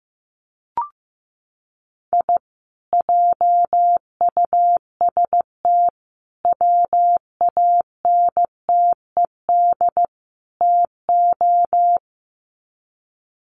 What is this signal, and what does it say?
Morse code, heard as a keyed tone, about 15 wpm, 700 Hz. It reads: I JUST WANTED TO
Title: Sets of 4 Words 15wpm